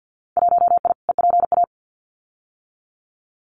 Morse code, heard as a keyed tone, about 50 wpm, 700 Hz. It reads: 1I EPA
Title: Winter Field Day Exchanges Encore 50wpm